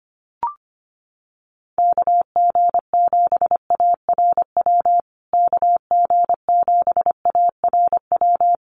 Morse code, heard as a keyed tone, about 25 wpm, 700 Hz. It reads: KG7ARW KG7ARW
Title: USA Call Signs Repeated 3x 25wpm